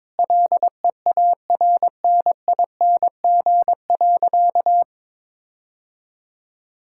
Morse code, heard as a keyed tone, about 21 wpm, 700 Hz. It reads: LEARNING.
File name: Sentences with 7-Word Phrases - 22wpm